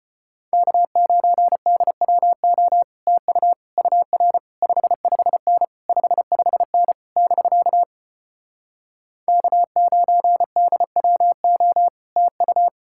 Morse code, heard as a keyed tone, about 33 wpm, 700 Hz. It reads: K9DWO TU UR 55N 55N <BK> K9DWO TU